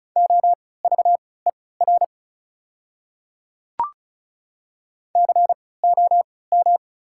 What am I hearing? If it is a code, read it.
Morse code, heard as a keyed tone, about 36 wpm, spaced as for 20 wpm, 690 Hz. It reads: OVER COM
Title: 3x Character Spacing - Top 500 Words - 35wpm